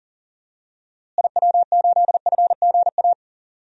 Morse code, heard as a keyed tone, about 40 wpm, 700 Hz. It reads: IW8FGA
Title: DXCC-100 Call Signs - 6 Characters - Repeated 3x - 40wpm